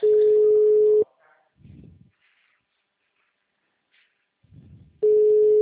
Ringback tone (Europe, 425 Hz).